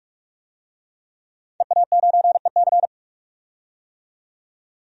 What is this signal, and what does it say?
Morse code, heard as a keyed tone, about 46 wpm, 700 Hz. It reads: EA9EC
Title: DXCC-100 Call Signs - 5 Characters - 45wpm